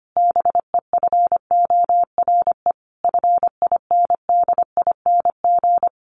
Morse code, heard as a keyed tone, about 25 wpm, 700 Hz. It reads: BEFORE FINDING
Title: Sentences from Top 1100 Words 25wpm